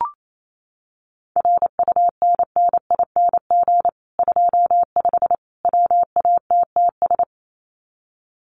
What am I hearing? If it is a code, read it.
Morse code, heard as a keyed tone, about 28 wpm, 700 Hz. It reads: RUNNING 25 WATTS
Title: Common QSO Phrases 28wpm